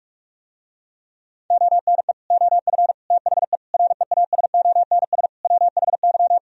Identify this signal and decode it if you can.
Morse code, heard as a keyed tone, about 49 wpm, 700 Hz. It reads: ONE OF THE REASONS WHY